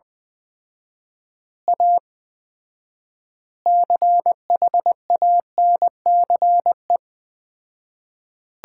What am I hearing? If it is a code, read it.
Morse code, heard as a keyed tone, about 20 wpm, 700 Hz. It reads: A CHANCE
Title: 4x Word-Spacing - Sets of 4 Words - 20wpm